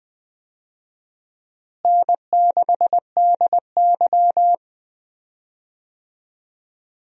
Morse code, heard as a keyed tone, about 20 wpm, 700 Hz. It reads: N6DY
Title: USA Call Signs - 4 Characters - Repeated 3x - 20wpm